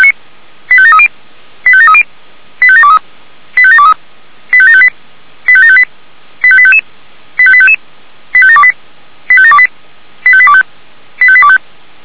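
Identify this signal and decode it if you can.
Five-tone selective call (ZVEI), heard as quick runs of five tones, tones of about 70 ms. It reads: 85730 85730 85723 85723 85758 85758 85855 85855 85728 85728 85825 85825
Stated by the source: ZVEI five-tone call 85719 received on SDR.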